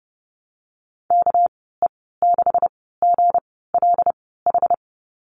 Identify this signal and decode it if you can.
Morse code, heard as a keyed tone, about 30 wpm, spaced as for 17 wpm, 700 Hz. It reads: KE6GLH
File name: Lesson 37g - Letter x - Call Signs 30wpm